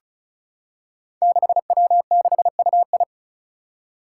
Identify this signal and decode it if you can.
Morse code, heard as a keyed tone, about 36 wpm, 700 Hz. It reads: 6W6UI
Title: Five Challenging Letter-Number Combinations 35wpm